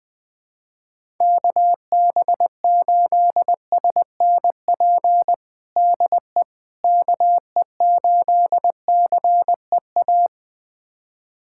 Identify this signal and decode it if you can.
Morse code, heard as a keyed tone, about 21 wpm, 700 Hz. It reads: KB8SNP DE KE8CEA
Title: USA Callsign DE Callsign - 6 Characters - 20wpm